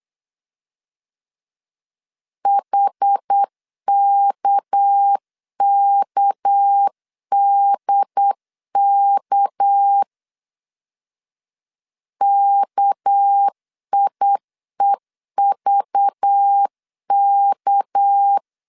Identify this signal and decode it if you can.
Morse code, heard as a keyed tone, about 8 wpm, 800 Hz. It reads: HKKDK KIEVK
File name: LEZIONI / 6 PKX0 / 6 PK 8wpm 3003